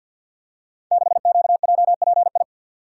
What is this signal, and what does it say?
Morse code, heard as a keyed tone, about 46 wpm, 700 Hz. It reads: 6QJPI
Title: Code Groups - 5 Letter:Numbers - 50wpm